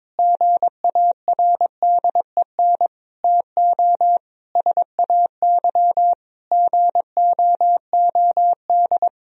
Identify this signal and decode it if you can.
Morse code, heard as a keyed tone, about 22 wpm, 700 Hz. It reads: GARDEN TO SAY GOOD
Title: Sentences from Top 1000 Words - Review Part 4 - 22wpm